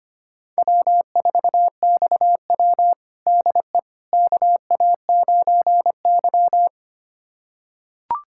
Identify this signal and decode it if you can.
Morse code, heard as a keyed tone, about 25 wpm, 700 Hz. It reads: W4XW DE KA9Y
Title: USA Callsign DE Callsign - 4 Characters - 25wpm